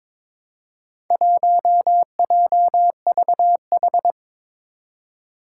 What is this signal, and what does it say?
Morse code, heard as a keyed tone, about 22 wpm, 700 Hz. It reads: 1JVH
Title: Four Challenging Letter-Number Combinations 22wpm